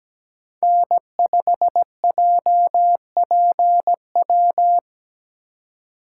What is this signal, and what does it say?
Morse code, heard as a keyed tone, about 17 wpm, 700 Hz. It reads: N5JPW
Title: USA Call Signs - 5 Characters - 3x Character Spacing - 17wpm